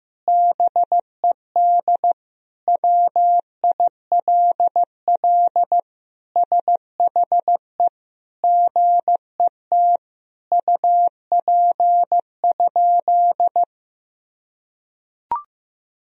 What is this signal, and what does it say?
Morse code, heard as a keyed tone, about 15 wpm, 700 Hz. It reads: BED WILL SHE GET UP?
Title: Sentences from Top 1000 Words - Review Part 1 - 15wpm